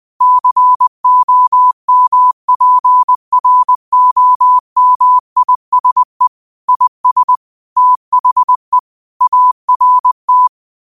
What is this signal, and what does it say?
Morse code, heard as a keyed tone, about 20 wpm, 1000 Hz. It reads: COMPROMISE IS THE ART
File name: qotdfiles / 20wpm